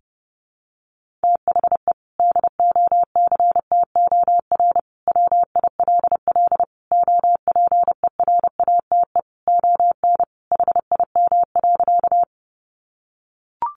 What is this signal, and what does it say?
Morse code, heard as a keyed tone, about 30 wpm, 700 Hz. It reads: THE DOCTOR WILL OPERATE ON HIM.
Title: Sentences from Top 1500 Words 30wpm